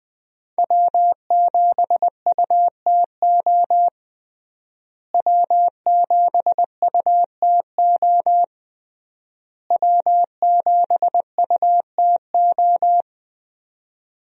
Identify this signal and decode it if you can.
Morse code, heard as a keyed tone, about 20 wpm, 700 Hz. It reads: W7UTO W7UTO W7UTO
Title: USA Call Signs - 5 Characters - Repeated 3x - 20wpm